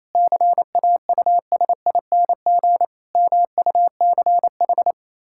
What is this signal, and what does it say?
Morse code, heard as a keyed tone, about 28 wpm, 700 Hz. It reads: CAUSING MUCH
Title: Pangram Challenge v2 - 28wpm